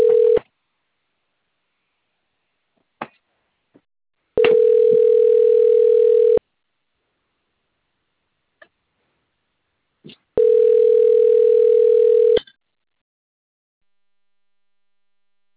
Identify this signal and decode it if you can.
Ringback tone (North America), 440 + 480 Hz.